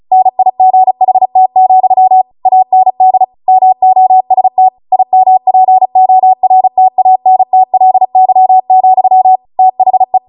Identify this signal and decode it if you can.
Morse code, heard as a keyed tone, about 36 wpm, 750 Hz. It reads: NIGHT, AND MOST IMPORTANTLY, THE
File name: Morse / 35WPM